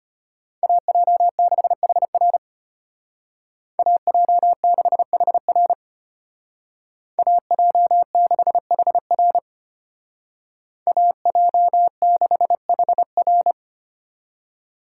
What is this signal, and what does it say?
Morse code, heard as a keyed tone, about 30 wpm, 700 Hz. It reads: AJ6HR AJ6HR AJ6HR AJ6HR
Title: Speed Racing - USA 2x2 Call Signs 25wpm